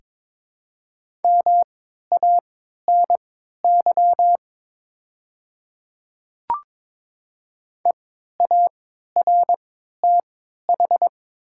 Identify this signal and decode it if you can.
Morse code, heard as a keyed tone, about 22 wpm, spaced as for 12 wpm, 700 Hz. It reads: MANY EARTH
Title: 3x Character Spacing - Top 200 Words - 22wpm